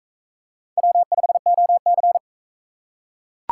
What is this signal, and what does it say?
Morse code, heard as a keyed tone, about 41 wpm, 700 Hz. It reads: W5OC